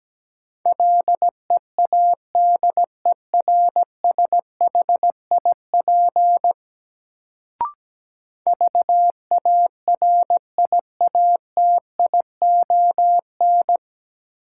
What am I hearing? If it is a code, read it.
Morse code, heard as a keyed tone, about 17 wpm, 700 Hz. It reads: LEADERSHIP VARIATION
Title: Sentences from Top 1600 Words - New Word Drill Alt - 17wpm